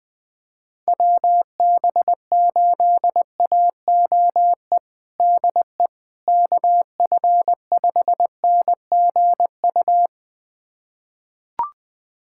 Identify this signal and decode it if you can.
Morse code, heard as a keyed tone, about 21 wpm, 700 Hz. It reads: WB8AOE DE KF5NGU
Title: USA Callsign DE Callsign - 6 Characters - 20wpm